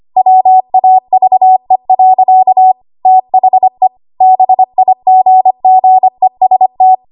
Morse code, heard as a keyed tone, about 25 wpm, 750 Hz. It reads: WAVE. THE BIGGEST
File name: Archive / 25 WPM / 25WPM